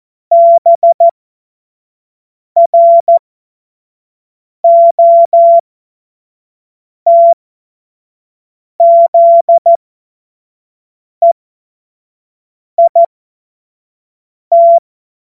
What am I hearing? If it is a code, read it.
Morse code, heard as a keyed tone, about 14 wpm, spaced as for 5 wpm, 680 Hz. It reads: BROTZEIT
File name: Witz-16-September-2025-25-F70